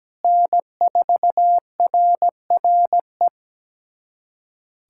Morse code, heard as a keyed tone, about 17 wpm, 700 Hz. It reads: N4RRE
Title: USA Callsign DE Callsign - 5 Characters - 17wpm